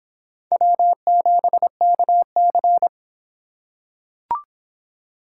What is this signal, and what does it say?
Morse code, heard as a keyed tone, about 26 wpm, 700 Hz. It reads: W7KC